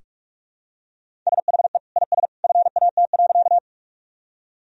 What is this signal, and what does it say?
Morse code, heard as a keyed tone, about 46 wpm, 700 Hz. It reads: SHE IS FAT.
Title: Sentences from Top 1000 Words - Review Part 4 - 45wpm